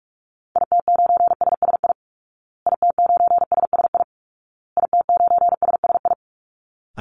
Morse code, heard as a keyed tone, about 44 wpm, 700 Hz. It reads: IT9SSI IT9SSI IT9SSI
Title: Call Signs Repeated 2x - 45wpm